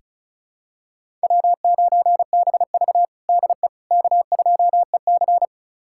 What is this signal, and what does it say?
Morse code, heard as a keyed tone, about 36 wpm, 700 Hz. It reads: W9BV DE K2EC
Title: USA Callsign DE Callsign - 4 Characters - 35wpm